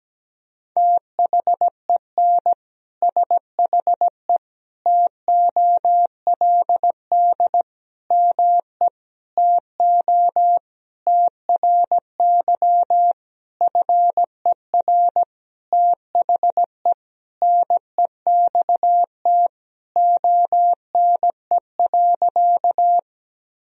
Morse code, heard as a keyed tone, about 17 wpm, 700 Hz. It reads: THEN SHE TOLD ME TO TRY FER THE NEXT ONE.
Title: Sentences from Top 400 Words and Abbreviations 17wpm